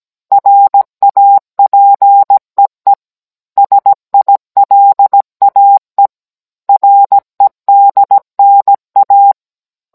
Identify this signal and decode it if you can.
Morse code, heard as a keyed tone, about 17 wpm, 800 Hz. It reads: RAPEE SILAE REDNA